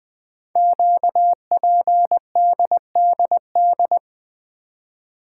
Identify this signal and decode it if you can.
Morse code, heard as a keyed tone, about 20 wpm, 700 Hz. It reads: QPDDD
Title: Code Groups - 5 Letters - 20wpm